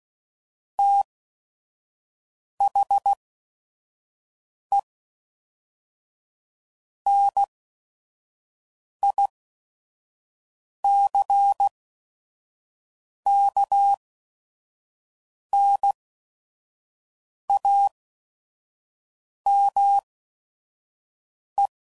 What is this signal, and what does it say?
Morse code, heard as a keyed tone, about 16 wpm, spaced as for 5 wpm, 800 Hz. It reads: THE NICKNAME
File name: CW Code Groups @5WPM